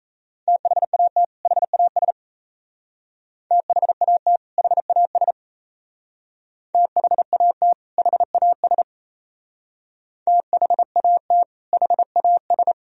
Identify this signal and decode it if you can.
Morse code, heard as a keyed tone, about 36 wpm, 700 Hz. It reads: THAT HAS THAT HAS THAT HAS THAT HAS